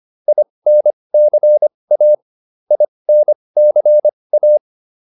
Morse code, heard as a keyed tone, about 25 wpm, spaced as for 20 wpm, 600 Hz. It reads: INCA INCA